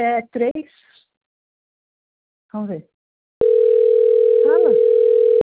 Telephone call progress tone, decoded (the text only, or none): ringback tone